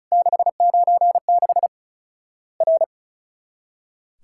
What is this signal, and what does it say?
Morse code, heard as a keyed tone, about 36 wpm, 700 Hz. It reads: 696 R
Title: Worked All Europe DX Contest - QTC Exchange - 35wpm